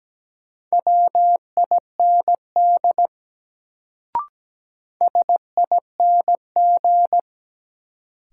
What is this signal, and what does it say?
Morse code, heard as a keyed tone, about 17 wpm, 700 Hz. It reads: WIND SING
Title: Sentences from Top 400 Words - New Word Drill - 17wpm